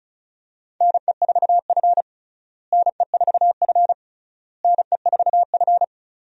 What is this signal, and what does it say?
Morse code, heard as a keyed tone, about 36 wpm, 700 Hz. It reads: NE4F NE4F NE4F
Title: USA Call Signs - 4 Characters - Repeated 3x - 35wpm